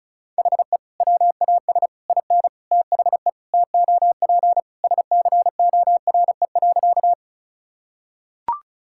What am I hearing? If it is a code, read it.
Morse code, heard as a keyed tone, about 34 wpm, 700 Hz. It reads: HE WAS IN THE TOP SCORE.
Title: Sentences from Top 1600 Words 35wpm